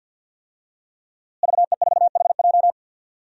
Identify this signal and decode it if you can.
Morse code, heard as a keyed tone, about 52 wpm, 700 Hz. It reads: VE4HJ
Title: DXCC-100 Call Signs - 5 Characters - Repeated 2x - 50wpm